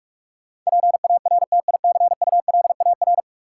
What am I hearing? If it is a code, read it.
Morse code, heard as a keyed tone, about 46 wpm, 700 Hz. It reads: PARTICULAR
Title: Sentences from Top 600 Words - New Word Drill - 45wpm